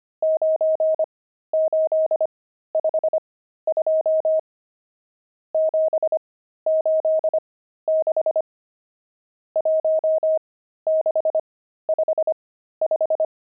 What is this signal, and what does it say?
Morse code, heard as a keyed tone, about 25 wpm, spaced as for 13 wpm, 630 Hz. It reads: 9852 786 1655
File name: pangram-fs-13-cs-25-nums17